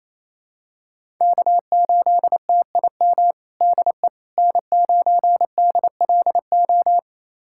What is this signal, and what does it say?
Morse code, heard as a keyed tone, about 28 wpm, 700 Hz. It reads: K8TIM DE N9DLO
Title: USA Callsign DE Callsign - 5 Characters - 28wpm